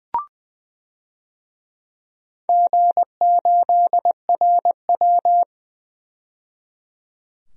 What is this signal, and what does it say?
Morse code, heard as a keyed tone, about 20 wpm, 700 Hz. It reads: G8RW
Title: DXCC-100 Call Signs - 4 Characters - 20wpm